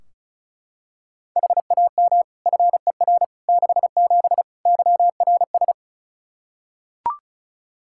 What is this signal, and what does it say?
Morse code, heard as a keyed tone, about 36 wpm, 700 Hz. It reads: HAM FER 67 YRS
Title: Practice QSO - Exchange 3 - More Info - 35wpm